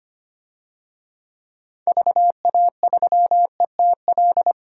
Morse code, heard as a keyed tone, about 25 wpm, 700 Hz. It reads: VA3ETL
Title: DXCC-100 Call Signs - 6 Characters - 25wpm